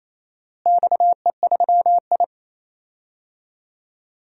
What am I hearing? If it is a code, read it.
Morse code, heard as a keyed tone, about 28 wpm, 700 Hz. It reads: XE3I
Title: DXCC-100 Call Signs - 4 Characters - Repeated 3x - 28wpm